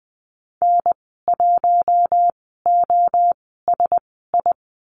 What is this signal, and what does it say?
Morse code, heard as a keyed tone, about 20 wpm, spaced as for 15 wpm, 700 Hz. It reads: N1OSI
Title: 055 Lesson 10g - Letter r - Call Signs 20wpm